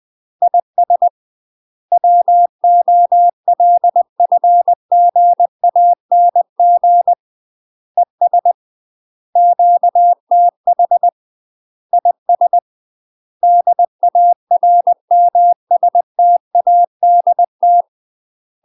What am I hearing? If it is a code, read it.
Morse code, heard as a keyed tone, about 20 wpm, 700 Hz. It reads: IS WOLFGANG ES QTH IS DARMSTADT
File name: UK MASTER QSOs TRAINING / 20WPM / UK MASTER QSOs 0018